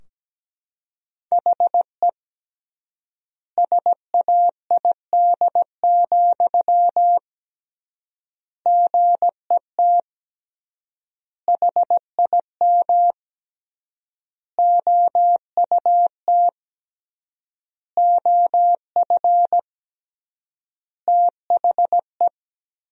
Morse code, heard as a keyed tone, about 17 wpm, 700 Hz. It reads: HE SAID, GET HIM OUT OF THE
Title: 3x Word-Spacing - Sentences from Top 400 Words - 17wpm